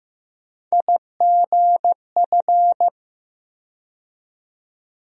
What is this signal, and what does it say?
Morse code, heard as a keyed tone, about 15 wpm, 700 Hz. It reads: IGF
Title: USA Call Sign Postfixes - 3 Letters 15wpm